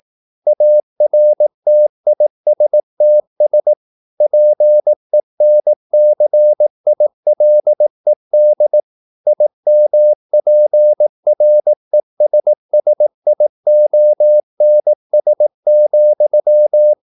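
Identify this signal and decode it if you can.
Morse code, heard as a keyed tone, about 18 wpm, 600 Hz. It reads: ARTISTS PENCILED IMPRESSIONS,
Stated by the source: Morse practice at 18 wpm, English text, from CW med Gnister.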